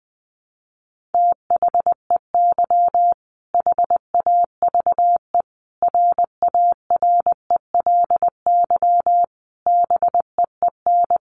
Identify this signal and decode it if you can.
Morse code, heard as a keyed tone, about 20 wpm, 700 Hz. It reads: THEY HAVE RARELY BEEN
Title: Sentences from Top 1400 Words 20wpm